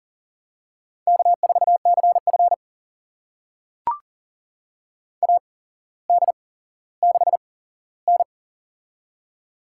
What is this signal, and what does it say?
Morse code, heard as a keyed tone, about 37 wpm, spaced as for 17 wpm, 700 Hz. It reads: K4CF AD6N
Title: USA Call Signs - 4 Characters - 8x Character Spacing - 40wpm